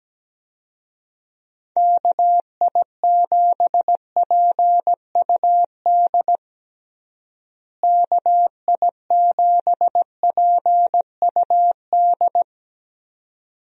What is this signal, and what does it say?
Morse code, heard as a keyed tone, about 17 wpm, 700 Hz. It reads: KI7PUD KI7PUD
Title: USA Call Signs - 6 Characters - Repeated 2x - 17wpm